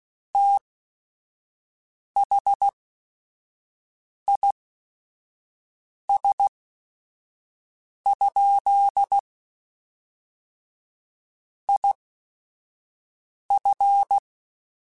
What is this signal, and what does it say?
Morse code, heard as a keyed tone, about 16 wpm, spaced as for 5 wpm, 800 Hz. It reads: THIS? IF